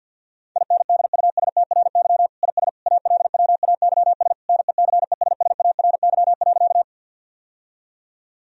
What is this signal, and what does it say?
Morse code, heard as a keyed tone, about 50 wpm, 700 Hz. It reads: INDUSTRY IS ALWAYS NECESSARY.
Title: Sentences from Top 1000 Words - Review Part 4 - 50wpm